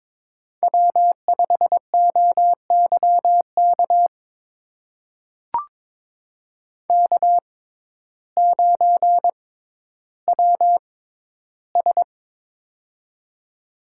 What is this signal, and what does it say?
Morse code, heard as a keyed tone, about 22 wpm, spaced as for 12 wpm, 700 Hz. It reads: W5OYK K9WS
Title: USA Call Signs - 6x Character Spacing - 22wpm